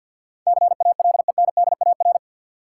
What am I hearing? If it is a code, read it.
Morse code, heard as a keyed tone, about 52 wpm, 700 Hz. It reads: CALENDAR
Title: Sentences from Top 1600 Words - New Word Drill Alt - 50wpm